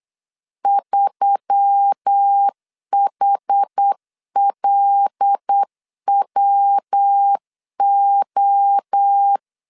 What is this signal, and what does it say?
Morse code, heard as a keyed tone, about 8 wpm, 800 Hz. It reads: 3HLWO
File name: LEZIONI / 6 PKX0 / 6 PKX 8wpm 3041